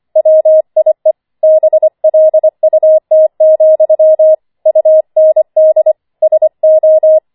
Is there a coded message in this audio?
Yes: Morse code from 0.2 to 7.2 s, heard as a keyed tone, about 25 wpm, 630 Hz. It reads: WIE BLUT, UND SO